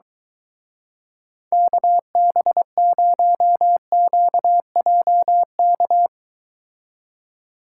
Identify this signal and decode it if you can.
Morse code, heard as a keyed tone, about 23 wpm, 700 Hz. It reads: KB0QJK